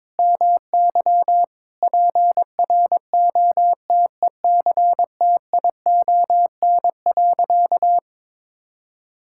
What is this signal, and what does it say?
Morse code, heard as a keyed tone, about 21 wpm, 700 Hz. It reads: MY PROTECTION.
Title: Sentences from Top 1600 Words 22wpm